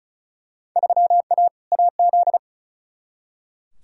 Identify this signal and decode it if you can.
Morse code, heard as a keyed tone, about 34 wpm, 700 Hz. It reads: 3A AZ
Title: Field Day Exchanges - 35wpm